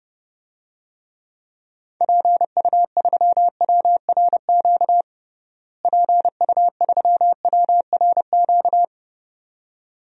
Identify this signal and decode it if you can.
Morse code, heard as a keyed tone, about 30 wpm, 700 Hz. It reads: PU3WRQ PU3WRQ
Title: DXCC-100 Call Signs - 6 Characters - Repeated 2x - 30wpm